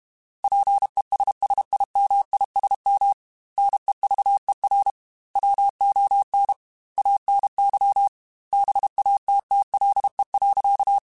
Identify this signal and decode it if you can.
Morse code, heard as a keyed tone, about 31 wpm, 800 Hz. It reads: PESSIMISM NEVER WON ANY BATTLE.
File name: Quotes / Quotes-mp3s / Pessimism @30